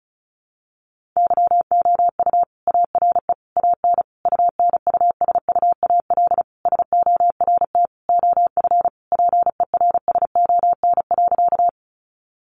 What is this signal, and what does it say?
Morse code, heard as a keyed tone, about 34 wpm, 700 Hz. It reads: YOU ARE AN UNUSUAL SORT OF PERSON.
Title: Sentences from Top 1400 Words 35wpm